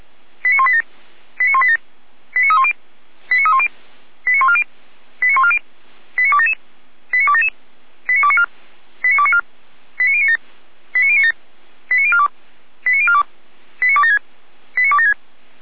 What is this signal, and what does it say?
Five-tone selective call (ZVEI), heard as quick runs of five tones, tones of about 70 ms. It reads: 89187 89187 89310 89310 89140 89140 89277 89277 89294 89294 89097 89097 89042 89042 89276 89276